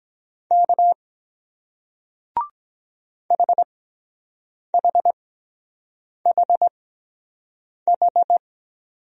Morse code, heard as a keyed tone, about 23 wpm, spaced as for 7 wpm, 700 Hz. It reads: K HHHH